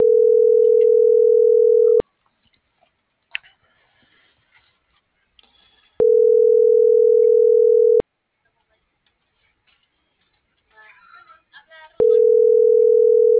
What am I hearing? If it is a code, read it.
Ringback tone (North America), 440 + 480 Hz.